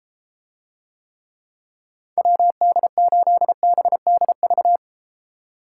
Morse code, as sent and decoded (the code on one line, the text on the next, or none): .-- -.. ---.. -... -.. ...-
WD8BDV